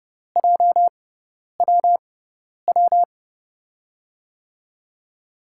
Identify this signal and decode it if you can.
Morse code, heard as a keyed tone, about 30 wpm, spaced as for 10 wpm, 700 Hz. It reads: JWW
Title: USA Call Signs - 5 Characters - 6x Character Spacing - 30wpm